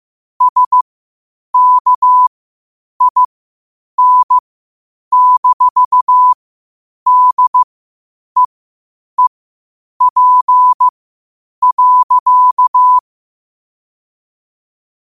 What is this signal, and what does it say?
Morse code, heard as a keyed tone, about 15 wpm, spaced as for 8 wpm, 1000 Hz. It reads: SKIN-DEEP.
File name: qotdfiles / 8wpm